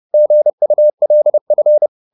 Morse code, heard as a keyed tone, about 30 wpm, 600 Hz. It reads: GULF